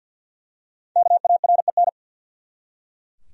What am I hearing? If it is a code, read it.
Morse code, heard as a keyed tone, about 48 wpm, 700 Hz. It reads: KAREN
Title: USA Common Names - 50wpm